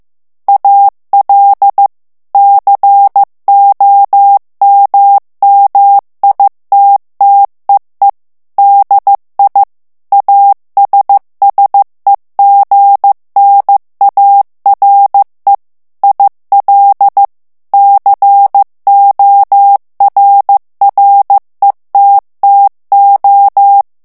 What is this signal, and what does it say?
Morse code, heard as a keyed tone, about 15 wpm, 800 Hz. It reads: AL COMMITTEE DI ASSEGNARE IL CORRETTO
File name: morse mp3 / ITALIEN texte clair 2 vitesse 14 mots minute